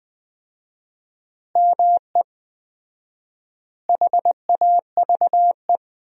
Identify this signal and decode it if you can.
Morse code, heard as a keyed tone, about 20 wpm, 700 Hz. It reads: ME HAVE
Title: 4x Word-Spacing - Sentences from Top 300 Words - 20wpm